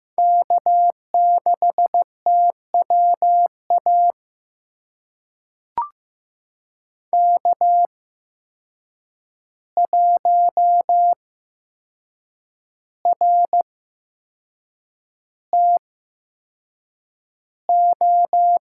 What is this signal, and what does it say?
Morse code, heard as a keyed tone, about 15 wpm, spaced as for 6 wpm, 700 Hz. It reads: K6TWA K1RTO
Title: USA Call Signs - 5 Characters - 8x Character Spacing - 15wpm